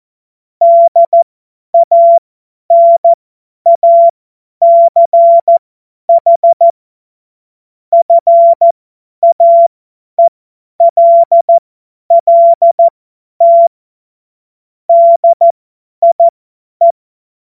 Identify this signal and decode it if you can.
Morse code, heard as a keyed tone, about 14 wpm, spaced as for 10 wpm, 680 Hz. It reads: DANACH FAELLT DIE
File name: Morsekurs / Witze / Witz 16. Dezember 2025 50 F70